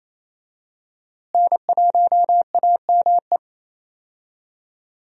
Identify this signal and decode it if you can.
Morse code, heard as a keyed tone, about 28 wpm, 700 Hz. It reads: N1AME
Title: USA Call Signs - 5 Characters - Repeated 2x - 28wpm